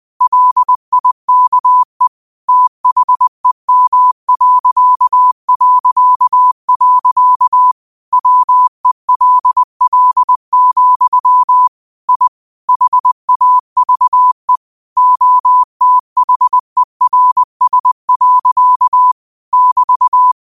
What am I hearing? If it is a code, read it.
Morse code, heard as a keyed tone, about 20 wpm, 1000 Hz. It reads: LIKE THEM... WELL, I HAVE OTHERS. <BT>
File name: qotdfiles / 20wpm